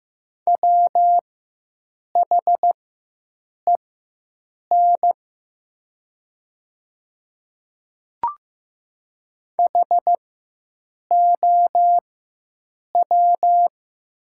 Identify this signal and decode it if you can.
Morse code, heard as a keyed tone, about 15 wpm, spaced as for 7 wpm, 700 Hz. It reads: WHEN HOW
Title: 4x Character Spacing - Top 200 Words - 15wpm